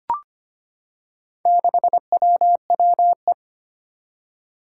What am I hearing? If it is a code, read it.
Morse code, heard as a keyed tone, about 25 wpm, 700 Hz. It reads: 6WWE
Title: Four Challenging Letter-Number Combinations 25wpm